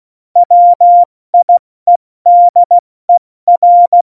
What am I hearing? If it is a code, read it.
Morse code, heard as a keyed tone, about 16 wpm, 700 Hz. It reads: WIEDER
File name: Morsekurs / Witze / Witz 10. Oktober 17, Tempo 70